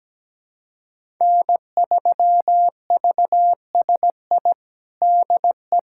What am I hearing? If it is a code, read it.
Morse code, heard as a keyed tone, about 17 wpm, 700 Hz. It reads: N3VSI DE
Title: USA Callsign DE Callsign - 5 Characters - 17wpm